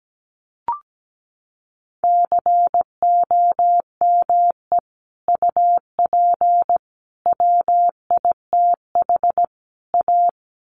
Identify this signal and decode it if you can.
Morse code, heard as a keyed tone, about 17 wpm, 700 Hz. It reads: COME UP WITH A
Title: Sets of 4 Words 17wpm